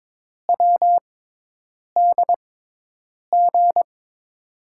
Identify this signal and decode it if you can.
Morse code, heard as a keyed tone, about 22 wpm, spaced as for 8 wpm, 700 Hz. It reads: WDG
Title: USA Call Signs - 6 Characters - 6x Character Spacing - 22wpm